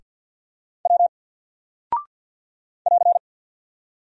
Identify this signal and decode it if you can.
Morse code, heard as a keyed tone, about 49 wpm, 700 Hz. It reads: W <AR>
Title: Single Character (Letter:Number:Punctuation:Prosign) 50wpm